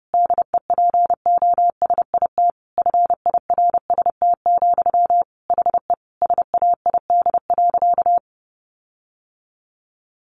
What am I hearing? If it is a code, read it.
Morse code, heard as a keyed tone, about 30 wpm, 700 Hz. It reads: DEPOSIT FIRST, HE SAID.
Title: Sentences from Top 1300 Words 30wpm